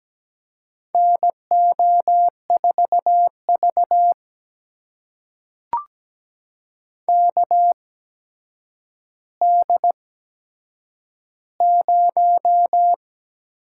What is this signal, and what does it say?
Morse code, heard as a keyed tone, about 17 wpm, spaced as for 8 wpm, 700 Hz. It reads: NO4V KD0
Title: USA Call Signs - 8x Character Spacing - 17wpm